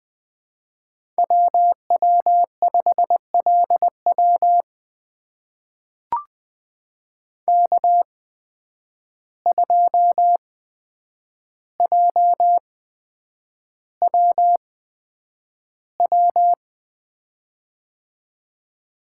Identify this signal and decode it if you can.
Morse code, heard as a keyed tone, about 20 wpm, spaced as for 9 wpm, 700 Hz. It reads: WW5LW K2JWW
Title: USA Call Signs - 5 Characters - 8x Character Spacing - 20wpm